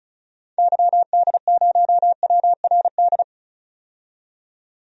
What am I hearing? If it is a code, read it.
Morse code, heard as a keyed tone, about 36 wpm, 700 Hz. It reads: YD0WRD
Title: DXCC-100 Call Signs - 6 Characters - 35wpm